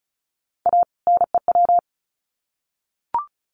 Morse code, heard as a keyed tone, about 34 wpm, 700 Hz. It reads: A NEW
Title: Sets of 2 Words 35wpm